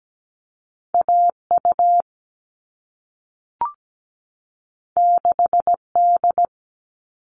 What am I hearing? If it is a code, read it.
Morse code, heard as a keyed tone, about 17 wpm, 700 Hz. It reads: AU 6D
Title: All Cut Numbers - 2-Digit 17wpm